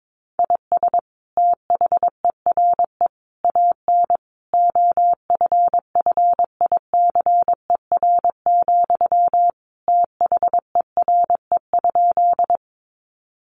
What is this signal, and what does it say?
Morse code, heard as a keyed tone, about 22 wpm, 700 Hz. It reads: IS THERE AN OFFICER, THERE?
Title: Sentences from Top 1200 Words 22wpm